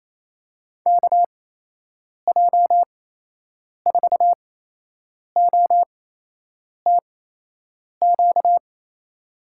Morse code, heard as a keyed tone, about 28 wpm, spaced as for 8 wpm, 700 Hz. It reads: KJ4OTQ
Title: USA Call Signs - 6 Characters - 8x Character Spacing - 28wpm